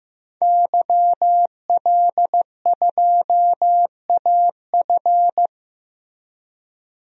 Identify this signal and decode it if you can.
Morse code, heard as a keyed tone, about 15 wpm, 700 Hz. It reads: YL2AF
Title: DXCC-100 Call Signs - 5 Characters - 15wpm